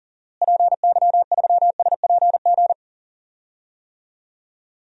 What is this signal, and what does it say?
Morse code, heard as a keyed tone, about 40 wpm, 700 Hz. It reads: PY3SPG
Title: DXCC-100 Call Signs - 6 Characters - Repeated 2x - 40wpm